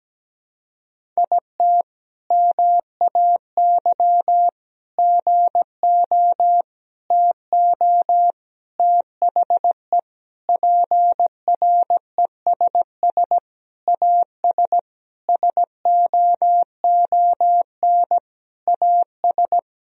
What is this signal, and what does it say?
Morse code, heard as a keyed tone, about 17 wpm, 700 Hz. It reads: IT MAY GO TO THE PRESS AS SOON AS
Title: Sentences from Top 1000 Words - Review Part 1 - 17wpm